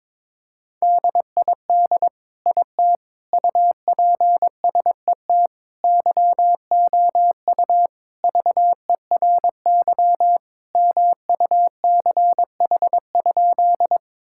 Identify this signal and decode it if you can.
Morse code, heard as a keyed tone, about 22 wpm, 700 Hz. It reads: DID IT UPSET YOU VERY MUCH?
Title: Sentences from Top 1000 Words - Review Part 5 - 22wpm